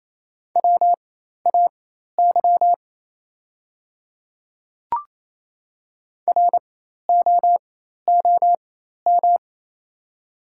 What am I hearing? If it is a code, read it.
Morse code, heard as a keyed tone, about 28 wpm, spaced as for 13 wpm, 700 Hz. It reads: WAY ROOM
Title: 4x Character Spacing - Top 500 Words - 28wpm